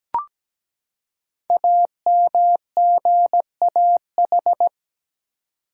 Morse code, heard as a keyed tone, about 17 wpm, 700 Hz. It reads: AMGAH
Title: Code Groups - 5 Letters - 17wpm